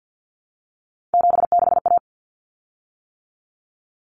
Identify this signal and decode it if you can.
Morse code, heard as a keyed tone, about 50 wpm, 700 Hz. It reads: 76A
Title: Common Cut Numbers - 3-Digit 50wpm